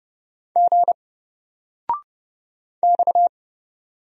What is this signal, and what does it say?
Morse code, heard as a keyed tone, about 30 wpm, 700 Hz. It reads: G X
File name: Single-Letter-Number-30wpm